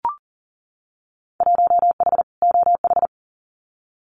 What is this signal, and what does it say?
Morse code, heard as a keyed tone, about 40 wpm, 700 Hz. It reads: 1H OH
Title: Winter Field Day Exchanges 40wpm